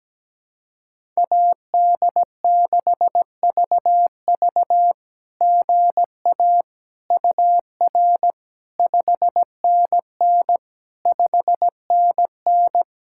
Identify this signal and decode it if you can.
Morse code, heard as a keyed tone, about 17 wpm, 700 Hz. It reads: AD6VV GA UR 5NN 5NN
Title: SOTA QSO Practice 17wpm